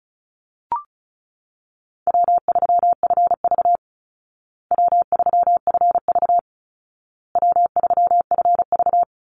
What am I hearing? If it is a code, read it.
Morse code, heard as a keyed tone, about 34 wpm, 700 Hz. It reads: W3FV W3FV W3FV
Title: Easy Call Signs Repeated 3x - 35wpm